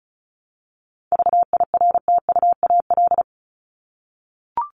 Morse code, heard as a keyed tone, about 34 wpm, 700 Hz. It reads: VIRTUAL
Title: LY Suffix - Top 100 Words with Base Words 35wpm